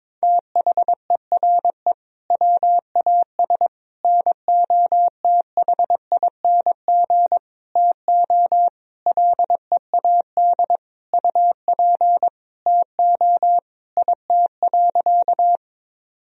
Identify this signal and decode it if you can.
Morse code, heard as a keyed tone, about 22 wpm, 700 Hz. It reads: THERE WAS NOTHING TO LEAD UP TO IT.
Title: Sentences from Top 1000 Words - Review Part 2 - 22wpm